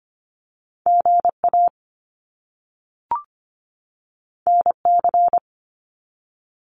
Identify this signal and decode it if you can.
Morse code, heard as a keyed tone, about 25 wpm, 700 Hz. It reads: GA NC
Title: US States 25wpm